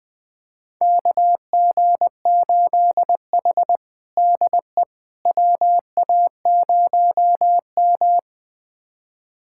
Morse code, heard as a keyed tone, about 21 wpm, 700 Hz. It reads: KG8H DE WA0M
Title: USA Callsign DE Callsign - 4 Characters - 20wpm